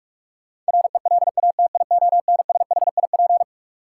Morse code, heard as a keyed tone, about 46 wpm, 700 Hz. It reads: RELATIONSHIP
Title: Sentences from Top 1100 Words - New Word Drill - 45wpm